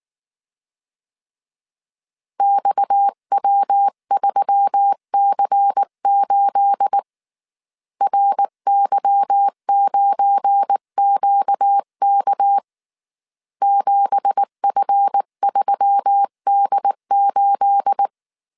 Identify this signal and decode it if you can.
Morse code, heard as a keyed tone, about 19 wpm, 800 Hz. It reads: XW3C8 RY9QK 7F3D8
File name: CW / DET / 1 9 1500C S C 18wpm 2539